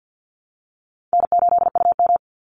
Morse code, heard as a keyed tone, about 50 wpm, 700 Hz. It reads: N8UM
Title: Call Signs Repeated 3x - 50wpm